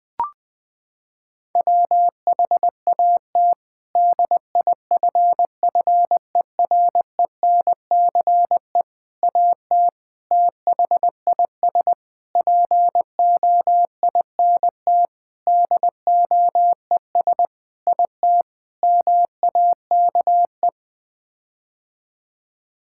Morse code, heard as a keyed tone, about 20 wpm, 700 Hz. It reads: WHAT DIFFERENCE AT THIS POINT DOES IT MAKE
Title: Sets of 8 Words - 20wpm